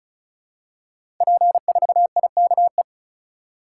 Morse code, heard as a keyed tone, about 36 wpm, 690 Hz. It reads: P4IKE
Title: Code Groups - 5 Letter:Numbers - 35wpm